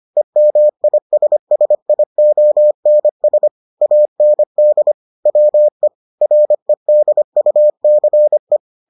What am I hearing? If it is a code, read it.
Morse code, heard as a keyed tone, about 25 wpm, 600 Hz. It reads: EMISSIONS AND WE REDUCE